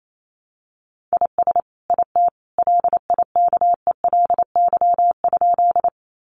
Morse code, heard as a keyed tone, about 28 wpm, 700 Hz. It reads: IS IT LIKELY?
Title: Sentences from Top 600 Words 28wpm